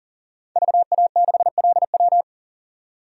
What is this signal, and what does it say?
Morse code, heard as a keyed tone, about 40 wpm, 700 Hz. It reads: VA6LW
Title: DXCC-100 Call Signs - 5 Characters - 40wpm